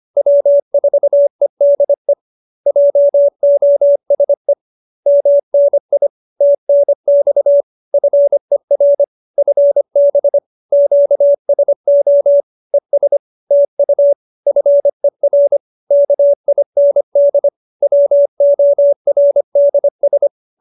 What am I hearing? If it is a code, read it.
Morse code, heard as a keyed tone, about 25 wpm, 570 Hz. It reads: W4EDE JOSE MNI TNX FER FB QSO ES TU FER KIND WORDS